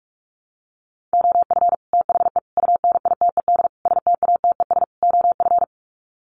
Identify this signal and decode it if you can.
Morse code, heard as a keyed tone, about 44 wpm, 700 Hz. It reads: OF THE UNITED STATES OF
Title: Sets of 5 Words Encore 45wpm